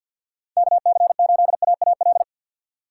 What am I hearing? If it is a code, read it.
Morse code, heard as a keyed tone, about 52 wpm, 700 Hz. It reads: KC8AAL
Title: USA Call Signs - 4x Character Spacing - 50wpm